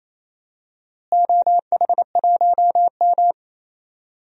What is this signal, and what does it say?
Morse code, heard as a keyed tone, about 28 wpm, 700 Hz. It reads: OH1M
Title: DXCC-100 Call Signs - 4 Characters - Repeated 3x - 28wpm